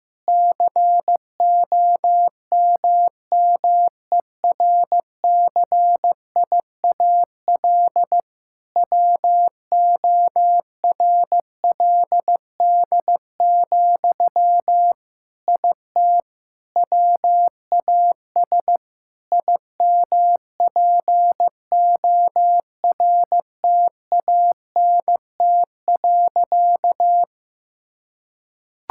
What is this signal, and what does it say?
Morse code, heard as a keyed tone, about 15 wpm, 700 Hz. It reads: COMMERCIAL WORLD, IT WAS IMPORTANT.
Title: Sentences from Top 1000 Words - Review Part 5 - 15wpm